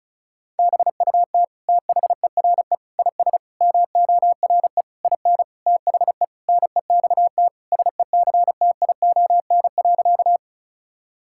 Morse code, heard as a keyed tone, about 36 wpm, 700 Hz. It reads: BUT THERE IS MORE IN THE NEXT SECTION.
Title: Sentences from Top 1000 Words - Review Part 5 - 35wpm